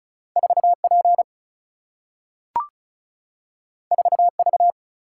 Morse code, heard as a keyed tone, about 36 wpm, 700 Hz. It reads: 4 P 4 V
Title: Single-Letter Single-Number 35wpm